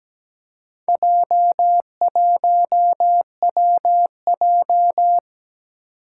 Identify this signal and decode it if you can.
Morse code, heard as a keyed tone, about 17 wpm, 700 Hz. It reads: J1WJ
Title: Four Challenging Letter-Number Combinations 17wpm